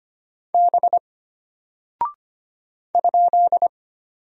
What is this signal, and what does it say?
Morse code, heard as a keyed tone, about 25 wpm, 700 Hz. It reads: B ?
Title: Single Character (Letter:Number:Punctuation:Prosign) Alt 25wpm